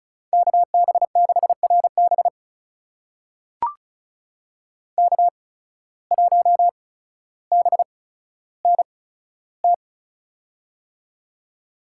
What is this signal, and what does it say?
Morse code, heard as a keyed tone, about 36 wpm, spaced as for 15 wpm, 700 Hz. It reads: KB6RB K1BNT
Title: USA Call Signs - 5 Characters - 8x Character Spacing - 35wpm